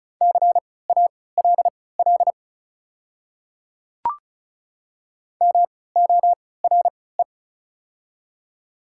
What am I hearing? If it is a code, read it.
Morse code, heard as a keyed tone, about 36 wpm, spaced as for 20 wpm, 700 Hz. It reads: CALL MORE
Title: 3x Character Spacing - Top 200 Words - 35wpm